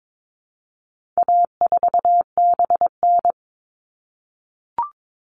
Morse code, heard as a keyed tone, about 22 wpm, 700 Hz. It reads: A4BN
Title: All Cut Numbers - 4-Digit 22wpm